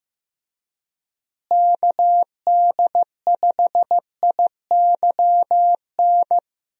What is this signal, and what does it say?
Morse code, heard as a keyed tone, about 15 wpm, 700 Hz. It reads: KD5IYN
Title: USA Call Signs - 6 Characters - Repeated 3x - 15wpm